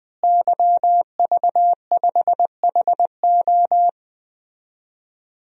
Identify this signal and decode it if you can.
Morse code, heard as a keyed tone, about 20 wpm, 700 Hz. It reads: YV5HO
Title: DXCC-100 Call Signs - 5 Characters - Repeated 2x - 20wpm